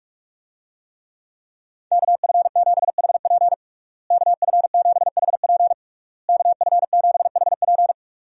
Morse code, heard as a keyed tone, about 46 wpm, 700 Hz. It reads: KF7HP KF7HP KF7HP
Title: USA Call Signs - 5 Characters - Repeated 3x - 45wpm